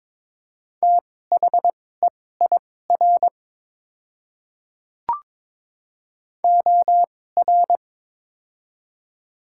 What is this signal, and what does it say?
Morse code, heard as a keyed tone, about 22 wpm, spaced as for 16 wpm, 700 Hz. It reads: THEIR OR
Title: 2x Character Spacing - Top 100 Words - 22wpm